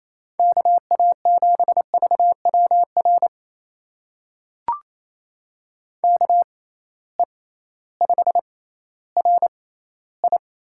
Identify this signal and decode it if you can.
Morse code, heard as a keyed tone, about 28 wpm, spaced as for 15 wpm, 700 Hz. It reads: KA7VWR KE5RI
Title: USA Call Signs - 6 Characters - 6x Character Spacing - 28wpm